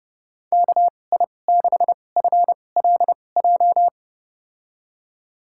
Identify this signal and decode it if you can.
Morse code, heard as a keyed tone, about 31 wpm, spaced as for 22 wpm, 700 Hz. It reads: KI6FLJ
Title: USA Call Signs - 6 Characters - 2x Character Spacing - 30wpm